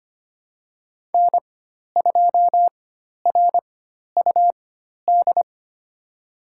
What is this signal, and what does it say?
Morse code, heard as a keyed tone, about 25 wpm, spaced as for 12 wpm, 700 Hz. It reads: N2RUD
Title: USA Call Signs - 5 Characters - 4x Character Spacing - 25wpm